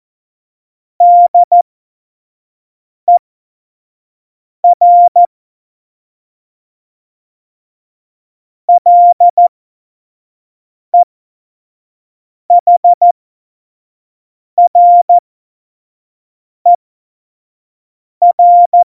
Morse code, heard as a keyed tone, about 14 wpm, spaced as for 5 wpm, 700 Hz. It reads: DER LEHRER
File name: Morsekurs / Witze / Witz 16. Juli 2019, 25 F70